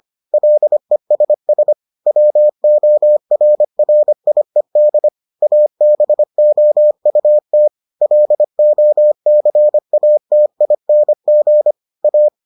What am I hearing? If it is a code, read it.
Morse code, heard as a keyed tone, about 25 wpm, 600 Hz. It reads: LESS WORRIED ABOUT LOCATING A